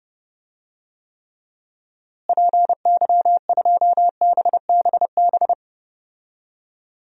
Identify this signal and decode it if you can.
Morse code, heard as a keyed tone, about 32 wpm, 700 Hz. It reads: PY2BBB
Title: DXCC-100 Call Signs - 6 Characters - Repeated 2x - 30wpm